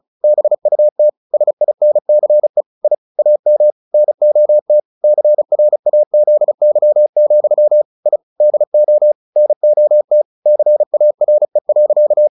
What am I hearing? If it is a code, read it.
Morse code, heard as a keyed tone, about 34 wpm, 600 Hz. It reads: BUT SINCE I AM NOT CRAZY, I DO NOT CARE.
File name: the old man and the sea / the old man and the sea 1 35wpm